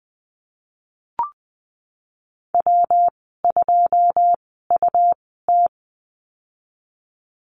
Morse code, heard as a keyed tone, about 20 wpm, spaced as for 15 wpm, 700 Hz. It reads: W2UT
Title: 207 Lesson 33g - Letter j - Call Signs 20wpm